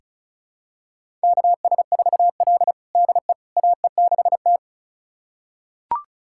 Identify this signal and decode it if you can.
Morse code, heard as a keyed tone, about 36 wpm, 700 Hz. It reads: KS4L DE AE6T
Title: USA Callsign DE Callsign - 4 Characters - 35wpm